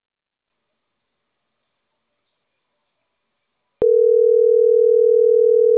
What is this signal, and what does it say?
Ringback tone (North America), 440 + 480 Hz.